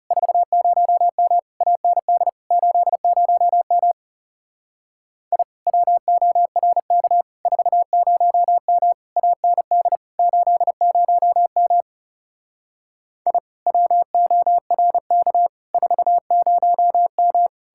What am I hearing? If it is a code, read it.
Morse code, heard as a keyed tone, about 39 wpm, 700 Hz. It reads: 40M AND 80M I WORK 40M AND 80M I WORK 40M